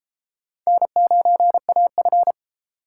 Morse code, heard as a keyed tone, about 29 wpm, 700 Hz. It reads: N9AF